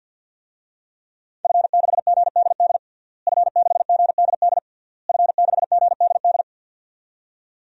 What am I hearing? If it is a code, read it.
Morse code, heard as a keyed tone, about 53 wpm, 700 Hz. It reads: F6GDD F6GDD F6GDD
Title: DXCC-100 Call Signs - 5 Characters - Repeated 3x - 50wpm